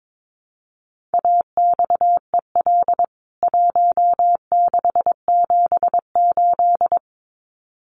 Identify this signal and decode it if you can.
Morse code, heard as a keyed tone, about 22 wpm, 700 Hz. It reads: AXEL 1678
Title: CWT Exchange 22wpm